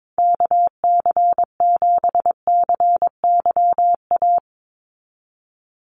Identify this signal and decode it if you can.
Morse code, heard as a keyed tone, about 22 wpm, 700 Hz. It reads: KC7CYA
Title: USA Call Signs 22wpm